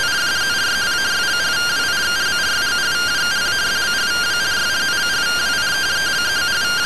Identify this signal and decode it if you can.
Two-tone teleprinter signal, about 170 Hz shift.